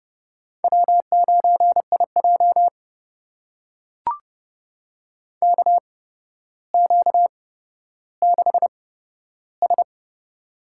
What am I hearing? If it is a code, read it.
Morse code, heard as a keyed tone, about 29 wpm, spaced as for 13 wpm, 700 Hz. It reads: W9IJ KQ6S
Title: USA Call Signs - 4 Characters - 8x Character Spacing - 30wpm